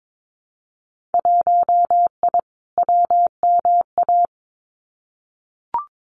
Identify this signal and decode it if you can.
Morse code, heard as a keyed tone, about 22 wpm, 700 Hz. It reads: 1I WMA
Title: Winter Field Day Exchanges Encore 22wpm